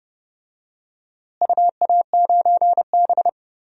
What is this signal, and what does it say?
Morse code, heard as a keyed tone, about 30 wpm, 700 Hz. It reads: UA9B
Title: DXCC-100 Call Signs - 4 Characters - Repeated 2x - 30wpm